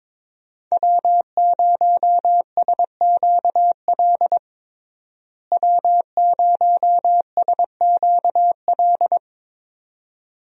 Morse code, heard as a keyed tone, about 22 wpm, 700 Hz. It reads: W0SQL W0SQL
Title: USA Call Signs - 5 Characters - Repeated 2x - 22wpm